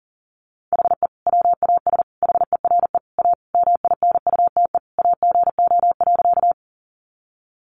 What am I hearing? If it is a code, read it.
Morse code, heard as a keyed tone, about 40 wpm, 700 Hz. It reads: HE WAS HERE A MINUTE AGO.
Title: Sentences from Top 500 Words 40wpm